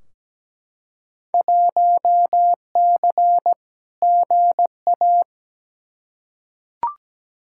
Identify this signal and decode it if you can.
Morse code, heard as a keyed tone, about 17 wpm, 700 Hz. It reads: 1C GA
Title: Field Day Exchanges Encore - 17wpm